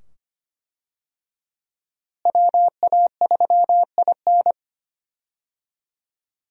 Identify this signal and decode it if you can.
Morse code, heard as a keyed tone, about 25 wpm, 700 Hz. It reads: WA3IN
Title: USA Call Signs - 5 Characters - Repeated 2x - 25wpm